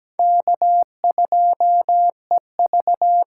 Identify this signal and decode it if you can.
Morse code, heard as a keyed tone, about 20 wpm, 700 Hz. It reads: K2EV
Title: Speed Racing - USA 1x2 Call Signs 17wpm